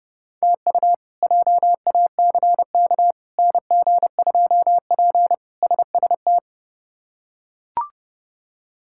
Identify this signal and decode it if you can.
Morse code, heard as a keyed tone, about 30 wpm, 700 Hz. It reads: TU JACK NG2P SST
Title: K1USN SST QSO Practice - 30wpm